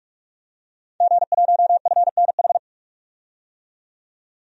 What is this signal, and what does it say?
Morse code, heard as a keyed tone, about 47 wpm, 700 Hz. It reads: G1FNH
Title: DXCC-100 Call Signs - 5 Characters - 45wpm